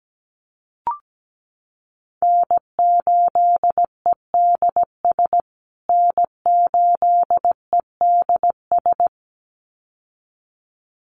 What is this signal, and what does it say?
Morse code, heard as a keyed tone, about 17 wpm, 700 Hz. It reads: N8EDS N8EDS
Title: USA Call Signs Repeated 2x 17wpm